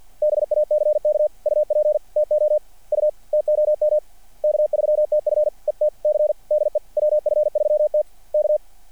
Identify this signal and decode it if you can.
Morse code, heard as a keyed tone, about 49 wpm, 600 Hz. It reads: BACK UP TO U TOM K3TF ET C DE WF3T K
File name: archived / CRSnet